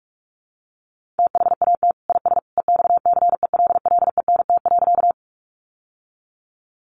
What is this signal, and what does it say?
Morse code, heard as a keyed tone, about 44 wpm, 700 Hz. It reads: THAT IS EXCELLENT.
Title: Sentences from Top 1000 Words 45wpm